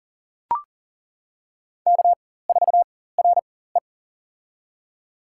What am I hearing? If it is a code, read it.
Morse code, heard as a keyed tone, about 40 wpm, spaced as for 19 wpm, 700 Hz. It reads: K4RE
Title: USA Call Signs - 4 Characters - 4x Character Spacing - 40wpm